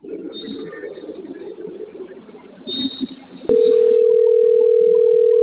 Ringback tone (North America), 440 + 480 Hz.